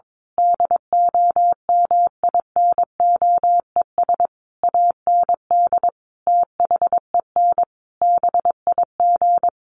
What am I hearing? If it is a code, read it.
Morse code, heard as a keyed tone, about 22 wpm, 700 Hz. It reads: DOMINOES AND THEN BIG
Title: A Sound Of Thunder 22wpm